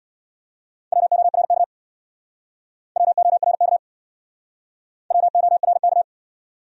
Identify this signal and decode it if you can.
Morse code, heard as a keyed tone, about 68 wpm, 700 Hz. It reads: WORK WORK WORK